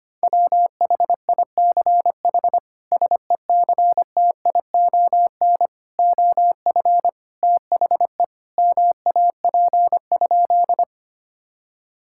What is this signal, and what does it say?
Morse code, heard as a keyed tone, about 25 wpm, 700 Hz. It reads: WHICH SECTION OF THE MAP?
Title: Sentences from Top 1000 Words - Review Part 5 - 25wpm